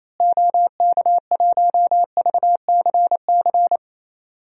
Morse code, heard as a keyed tone, about 28 wpm, 700 Hz. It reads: OK1VCC
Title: DXCC-100 Call Signs - 6 Characters - Repeated 2x - 28wpm